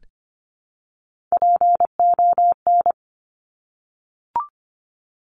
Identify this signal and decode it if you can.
Morse code, heard as a keyed tone, about 25 wpm, 700 Hz. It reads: PON
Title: Top Three Letter Combinations in Words 25wpm